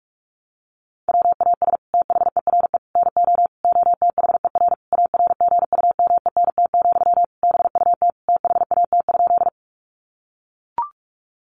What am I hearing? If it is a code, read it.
Morse code, heard as a keyed tone, about 44 wpm, 700 Hz. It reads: WAS THERE NO OTHER ARGUMENT, BUT THAT?
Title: Sentences from Top 1400 Words 45wpm